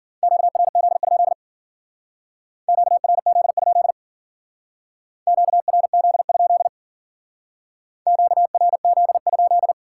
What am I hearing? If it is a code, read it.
Morse code, heard as a keyed tone, about 50 wpm, 700 Hz. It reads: QRZ? QRZ? QRZ? QRZ?